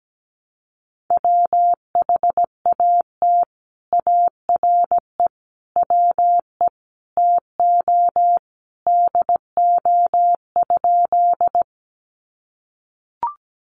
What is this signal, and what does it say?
Morse code, heard as a keyed tone, about 17 wpm, 700 Hz. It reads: WHAT ARE WE TO DO?
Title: Sentences from Top 100 Words 17wpm